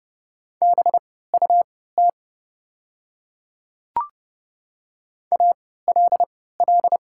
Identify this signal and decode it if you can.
Morse code, heard as a keyed tone, about 30 wpm, spaced as for 17 wpm, 690 Hz. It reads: BUT ALL
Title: 3x Character Spacing - Top 200 Words - 30wpm